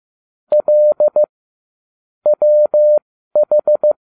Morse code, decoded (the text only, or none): L WH